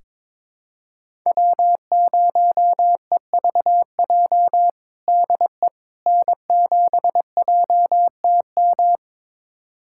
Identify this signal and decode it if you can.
Morse code, heard as a keyed tone, about 21 wpm, 700 Hz. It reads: W0EVJ DE N7JTM
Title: USA Callsign DE Callsign - 5 Characters - 22wpm